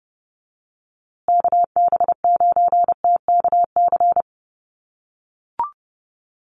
Morse code, heard as a keyed tone, about 30 wpm, 700 Hz. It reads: KB9TKC
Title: USA Call Signs Repeated 2x 30wpm